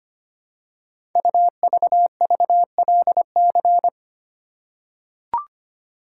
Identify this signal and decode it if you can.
Morse code, heard as a keyed tone, about 25 wpm, 700 Hz. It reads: UVVLC
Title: Code Groups - 5 Letters - 25wpm